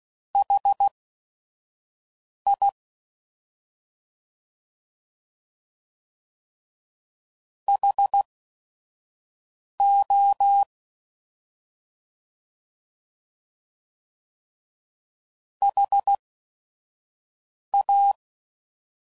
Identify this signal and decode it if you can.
Morse code, heard as a keyed tone, about 16 wpm, spaced as for 5 wpm, 800 Hz. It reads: HI HO HA